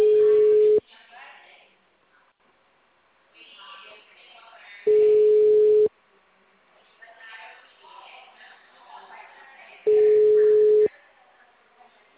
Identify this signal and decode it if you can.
Ringback tone (Europe, 425 Hz), with speech.